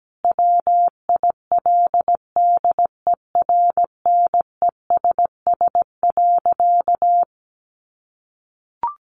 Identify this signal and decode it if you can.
Morse code, heard as a keyed tone, about 17 wpm, 700 Hz. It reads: WILDERNESS.
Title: A Sound Of Thunder 17wpm